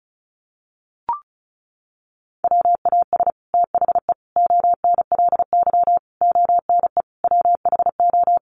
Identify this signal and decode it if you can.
Morse code, heard as a keyed tone, about 34 wpm, 700 Hz. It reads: WAS THE ONLY ONE WHO
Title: Sets of 5 Words Encore2 35wpm